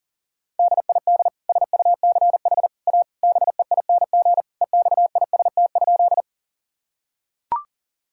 Morse code, heard as a keyed tone, about 40 wpm, 700 Hz. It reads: DID SUCH A BEING EXIST?
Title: Sentences from Top 1000 Words - Review Part 5 - 40wpm